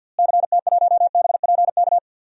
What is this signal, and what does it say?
Morse code, heard as a keyed tone, about 45 wpm, 700 Hz. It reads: CT1BPK
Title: DXCC-100 Call Signs - 6 Characters - Repeated 3x - 50wpm